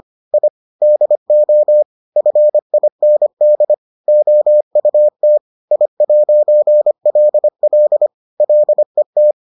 Morse code, heard as a keyed tone, about 25 wpm, 600 Hz. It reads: I DO FIND OUT I'LL LET